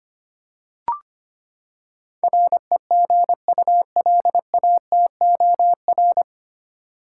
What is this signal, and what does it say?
Morse code, heard as a keyed tone, about 25 wpm, 700 Hz. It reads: REGULATOR
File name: Nine Letter Words - 25wpm